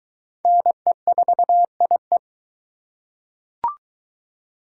Morse code, heard as a keyed tone, about 21 wpm, 700 Hz. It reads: NE4IE